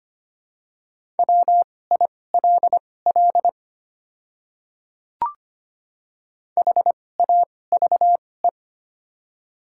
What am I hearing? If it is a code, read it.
Morse code, heard as a keyed tone, about 25 wpm, spaced as for 18 wpm, 690 Hz. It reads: WILL HAVE
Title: 2x Character Spacing - Top 100 Words - 25wpm